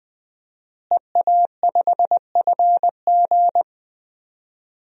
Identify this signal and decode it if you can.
Morse code, heard as a keyed tone, about 20 wpm, 700 Hz. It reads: EA5FG
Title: DXCC-100 Call Signs - 5 Characters - Repeated 2x - 20wpm